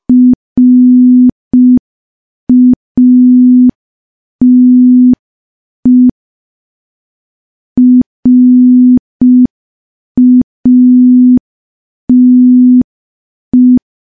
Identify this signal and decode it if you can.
Morse code, heard as a keyed tone, about 5 wpm, 250 Hz. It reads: RATE RATE
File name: morse / morse-training-05WPM